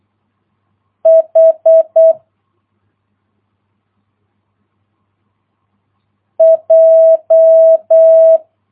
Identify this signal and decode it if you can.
Morse code, heard as a keyed tone, about 8 wpm, 650 Hz. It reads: H J